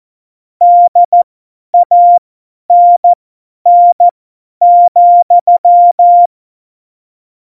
Morse code, heard as a keyed tone, about 14 wpm, spaced as for 10 wpm, 700 Hz. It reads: DANN,
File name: Morsekurs / Witze / Witz 02. Oktober 2018, 50 F70